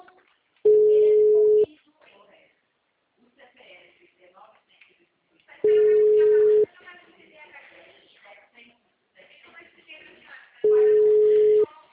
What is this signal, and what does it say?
Ringback tone (Europe, 425 Hz), with speech.